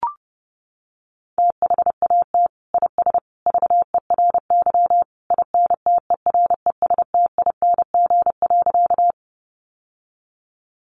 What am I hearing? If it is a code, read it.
Morse code, heard as a keyed tone, about 30 wpm, 700 Hz. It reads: THAT IS VERY INTERESTING.
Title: Sentences from Top 1000 Words 30wpm